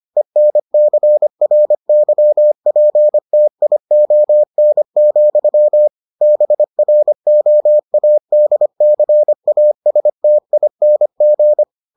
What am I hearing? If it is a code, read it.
Morse code, heard as a keyed tone, about 25 wpm, 600 Hz. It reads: ENCRYPTION, BROADCASTING